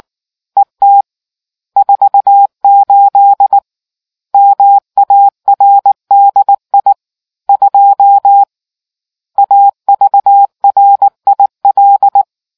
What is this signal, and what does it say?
Morse code, heard as a keyed tone, about 19 wpm, 800 Hz. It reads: ET 48 MARDI 2 AVRIL